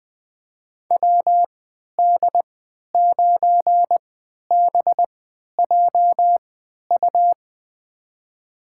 Morse code, heard as a keyed tone, about 20 wpm, spaced as for 11 wpm, 700 Hz. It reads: WD9BJU
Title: USA Call Signs - 6 Characters - 3x Character Spacing - 20wpm